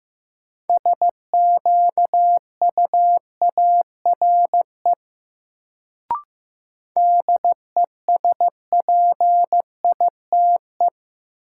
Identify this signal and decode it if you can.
Morse code, heard as a keyed tone, about 16 wpm, 700 Hz. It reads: SQUARE DESPITE
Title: Sentences from Top 1000 Words - Review Part 7 - New Word Drill - 15wpm